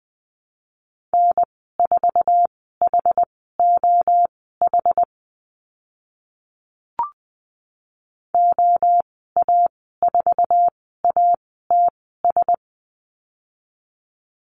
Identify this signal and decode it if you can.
Morse code, heard as a keyed tone, about 20 wpm, spaced as for 15 wpm, 700 Hz. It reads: N4HOH OA4ATS
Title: 088 Lesson 15g - Number 5 - Call Signs 20wpm